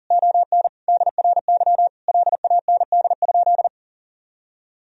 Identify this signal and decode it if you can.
Morse code, heard as a keyed tone, about 40 wpm, 700 Hz. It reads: ON DRY LAND?
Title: Sentences from Top 1000 Words - Review Part 3 - 40wpm